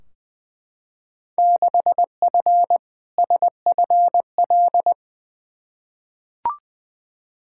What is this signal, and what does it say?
Morse code, heard as a keyed tone, about 20 wpm, 700 Hz. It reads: 6F SFL
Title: Field Day Exchanges Encore - 20wpm